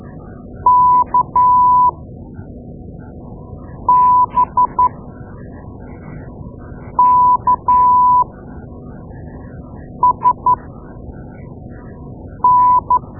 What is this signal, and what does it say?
Morse code, heard as a keyed tone, about 12 wpm, spaced as for 4 wpm, 990 Hz. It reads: KBKSN
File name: Morse Lessons / Morse Lessons / Tape-2A